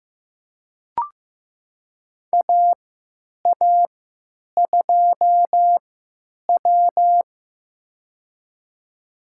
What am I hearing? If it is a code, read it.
Morse code, heard as a keyed tone, about 15 wpm, spaced as for 9 wpm, 700 Hz. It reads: AA2W
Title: USA Call Signs - 3x Character Spacing - 15wpm